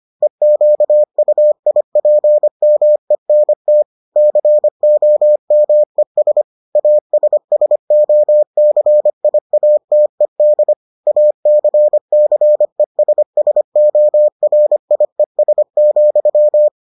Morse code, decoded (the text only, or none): EQUIPMENT COMES ASSOCIATED ACCESSORIES,